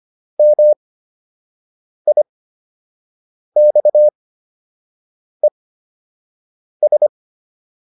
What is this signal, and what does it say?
Morse code, heard as a keyed tone, about 25 wpm, spaced as for 6 wpm, 600 Hz. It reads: MIXES